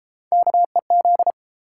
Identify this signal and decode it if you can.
Morse code, heard as a keyed tone, about 33 wpm, 700 Hz. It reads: KEZ